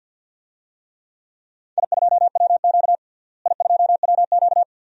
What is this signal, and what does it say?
Morse code, heard as a keyed tone, about 48 wpm, 700 Hz. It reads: I2WQ I2WQ
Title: DXCC-100 Call Signs - 4 Characters - Repeated 2x - 50wpm